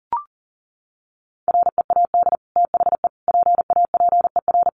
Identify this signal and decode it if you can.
Morse code, heard as a keyed tone, about 40 wpm, 700 Hz. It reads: READ THE PAPER
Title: Sentences from Top 600 Words 40wpm